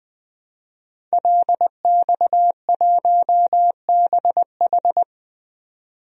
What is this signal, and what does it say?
Morse code, heard as a keyed tone, about 20 wpm, 700 Hz. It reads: LX1BH
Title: DXCC-100 Call Signs - 5 Characters - 20wpm